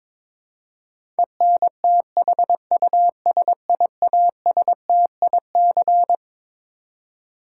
Morse code, heard as a keyed tone, about 22 wpm, 700 Hz. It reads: ENTHUSIASTIC
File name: Twelve Letter Words - 22wpm